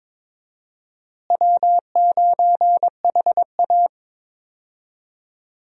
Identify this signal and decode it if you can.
Morse code, heard as a keyed tone, about 22 wpm, 700 Hz. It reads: W9HA
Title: USA Call Signs - 4 Characters - Repeated 3x - 22wpm